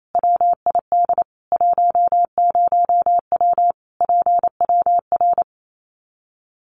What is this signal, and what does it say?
Morse code, heard as a keyed tone, about 28 wpm, 700 Hz. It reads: WID 10W PWR
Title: Common QSO Phrases 28wpm